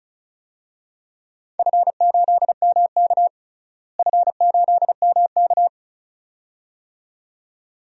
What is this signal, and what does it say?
Morse code, heard as a keyed tone, about 36 wpm, 700 Hz. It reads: F8MK F8MK
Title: DXCC-100 Call Signs - 4 Characters - Repeated 2x - 35wpm